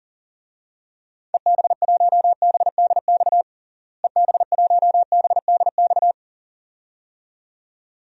Morse code, heard as a keyed tone, about 40 wpm, 700 Hz. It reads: EB1BDX EB1BDX
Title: DXCC-100 Call Signs - 6 Characters - Repeated 2x - 40wpm